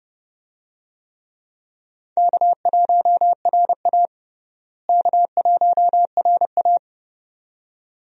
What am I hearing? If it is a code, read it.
Morse code, heard as a keyed tone, about 30 wpm, 700 Hz. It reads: K1RA K1RA
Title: USA Call Signs - 4 Characters - Repeated 2x - 30wpm